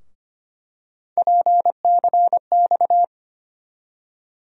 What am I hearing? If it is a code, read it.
Morse code, heard as a keyed tone, about 25 wpm, 700 Hz. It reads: PCX
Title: USA Call Sign Postfixes - 3 Letters 25wpm